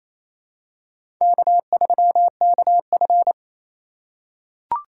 Morse code, heard as a keyed tone, about 28 wpm, 700 Hz. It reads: K3KF
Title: USA Call Signs - 4 Characters - 3x Character Spacing - 28wpm